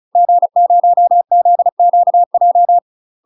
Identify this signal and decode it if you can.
Morse code, heard as a keyed tone, about 36 wpm, 700 Hz. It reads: G0ZQJ
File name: UK MASTER QSOs TRAINING / 35WPM / UK MASTER QSOs 0016